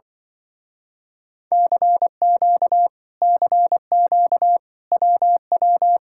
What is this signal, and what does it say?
Morse code, heard as a keyed tone, about 24 wpm, 700 Hz. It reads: CQ CQ WW
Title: Speed-Racing Common QSO Phrases - 20wpm